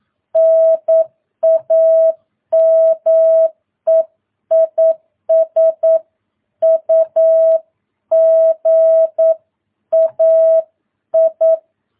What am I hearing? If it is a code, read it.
Morse code, heard as a keyed tone, about 9 wpm, 650 Hz. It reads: NAME IS UGAI
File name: jh8cbh / 2022cw